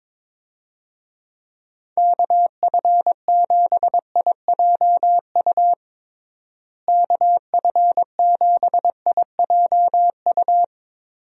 Morse code, heard as a keyed tone, about 22 wpm, 700 Hz. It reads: KF7IJU KF7IJU
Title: USA Call Signs - 6 Characters - Repeated 3x - 22wpm